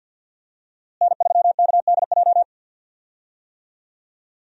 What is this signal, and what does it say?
Morse code, heard as a keyed tone, about 50 wpm, 700 Hz. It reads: N3KDJ
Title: USA Call Signs - 5 Characters - Repeated 2x - 50wpm